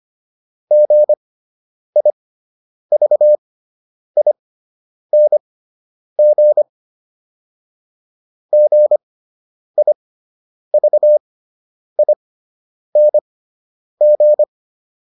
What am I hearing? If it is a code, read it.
Morse code, heard as a keyed tone, about 25 wpm, spaced as for 9 wpm, 600 Hz. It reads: GIVING GIVING
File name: ING6 09